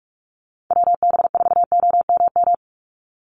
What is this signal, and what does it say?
Morse code, heard as a keyed tone, about 44 wpm, 700 Hz. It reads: WB4OMM
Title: Call Signs Repeated 2x - 45wpm